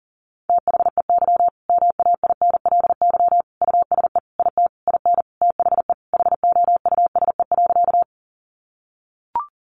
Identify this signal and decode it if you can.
Morse code, heard as a keyed tone, about 40 wpm, 700 Hz. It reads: THEY MAINLY USE IT IN THE HOUSE.
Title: Sentences from Top 1400 Words 40wpm